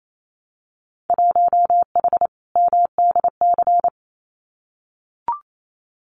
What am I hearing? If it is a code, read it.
Morse code, heard as a keyed tone, about 28 wpm, 700 Hz. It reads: 1H MDC
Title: Winter Field Day Exchanges 28wpm